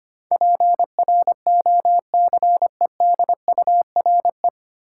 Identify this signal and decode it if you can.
Morse code, heard as a keyed tone, about 25 wpm, 700 Hz. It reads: PROCEDURE
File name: Nine Letter Words - 25wpm